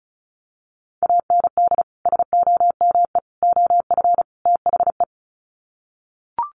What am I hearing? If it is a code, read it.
Morse code, heard as a keyed tone, about 34 wpm, 700 Hz. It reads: AND SOME OF THE
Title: Sets of 4 Words 35wpm